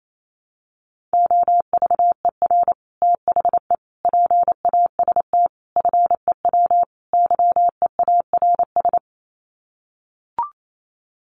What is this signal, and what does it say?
Morse code, heard as a keyed tone, about 28 wpm, 700 Hz. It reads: OVER THE PAST FEW YEARS
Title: Sets of 5 Words Encore 28wpm